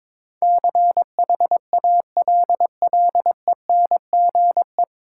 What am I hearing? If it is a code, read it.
Morse code, heard as a keyed tone, about 22 wpm, 700 Hz. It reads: CHALLENGE
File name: 9 Letter Words - First 3 Letters Given - 22wpm